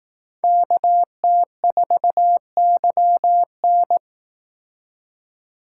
Morse code, heard as a keyed tone, about 18 wpm, 700 Hz. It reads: KT4YN
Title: Speed Racing - USA 2x2 Call Signs 15wpm